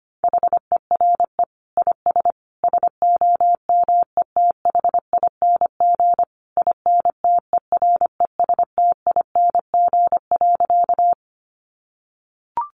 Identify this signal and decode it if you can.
Morse code, heard as a keyed tone, about 25 wpm, 700 Hz. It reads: HERE IS SOMETHING INTERESTING.
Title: Sentences from Top 1000 Words 25wpm